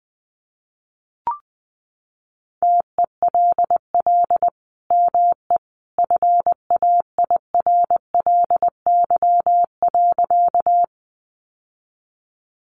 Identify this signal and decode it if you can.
Morse code, heard as a keyed tone, about 20 wpm, 700 Hz. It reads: TELL ME FAIRLY.
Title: Sentences from Top 1100 Words 20wpm